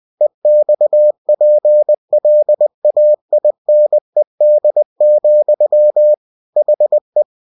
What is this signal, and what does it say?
Morse code, heard as a keyed tone, about 20 wpm, 600 Hz. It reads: EXPLAINED, HE
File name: cw-books / winnie-the-pooh / winnie-20wpm-page-0035